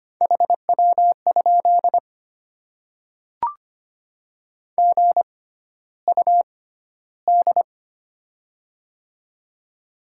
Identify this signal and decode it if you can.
Morse code, heard as a keyed tone, about 25 wpm, spaced as for 13 wpm, 700 Hz. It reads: HW? GUD
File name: Top 20 QSO Elements - 6x Character Spacing - 25wpm